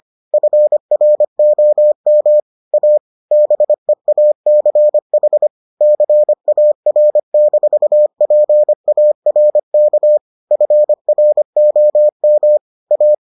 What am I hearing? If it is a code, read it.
Morse code, heard as a keyed tone, about 25 wpm, 600 Hz. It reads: FROM A BEACH CAR-PARK FROM A